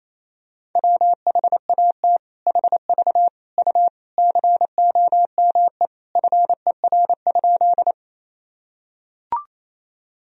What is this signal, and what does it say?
Morse code, heard as a keyed tone, about 28 wpm, 700 Hz. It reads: WHAT HV U COME FER?
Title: Sentences from Top 100 Words and Abbreviations 28wpm